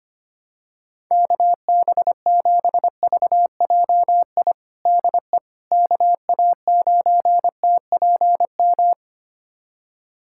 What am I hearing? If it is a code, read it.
Morse code, heard as a keyed tone, about 25 wpm, 700 Hz. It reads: KB7VJI DE KA9TPM
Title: USA Callsign DE Callsign - 6 Characters - 25wpm